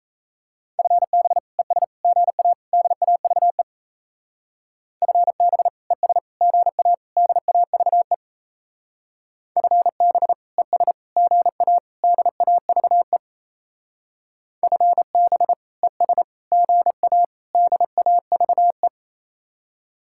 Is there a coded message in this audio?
Yes: Morse code from 0.8 to 18.9 s, heard as a keyed tone, about 36 wpm, 700 Hz. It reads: FB ES GA DAVE FB ES GA DAVE FB ES GA DAVE FB ES GA DAVE